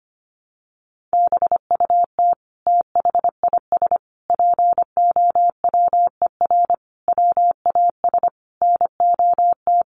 Morse code, heard as a keyed tone, about 25 wpm, 700 Hz. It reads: BUT THIS POWER WAS NOT
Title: Sentences from Top 1500 Words 25wpm